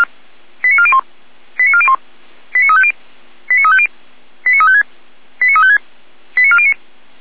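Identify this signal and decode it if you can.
Five-tone selective call (ZVEI), heard as quick runs of five tones, tones of about 70 ms. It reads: 89401 89401 89350 89350 89356 89356 89409
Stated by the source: ZVEI five-tone call received by SDR.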